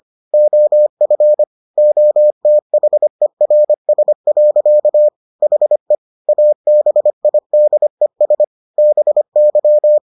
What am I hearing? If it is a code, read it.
Morse code, heard as a keyed tone, about 25 wpm, 600 Hz. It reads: OF OTHERS. HE ABIDES BY